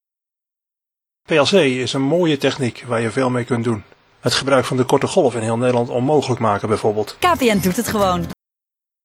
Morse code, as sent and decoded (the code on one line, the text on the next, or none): none